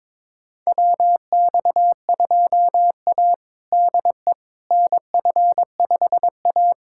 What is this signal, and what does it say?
Morse code, heard as a keyed tone, about 22 wpm, 700 Hz. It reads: WX2A DE NF5A
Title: USA Callsign DE Callsign - 4 Characters - 22wpm